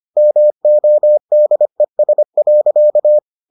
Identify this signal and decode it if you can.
Morse code, heard as a keyed tone, about 25 wpm, 600 Hz. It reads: MODES.